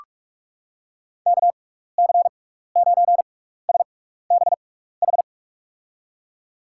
Morse code, heard as a keyed tone, about 46 wpm, spaced as for 15 wpm, 700 Hz. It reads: KC9SBH
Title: USA Call Signs - 6 Characters - 6x Character Spacing - 45wpm